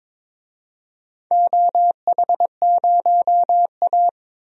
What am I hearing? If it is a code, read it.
Morse code, heard as a keyed tone, about 22 wpm, 700 Hz. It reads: OH0A
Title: DXCC-100 Call Signs - 4 Characters - 22wpm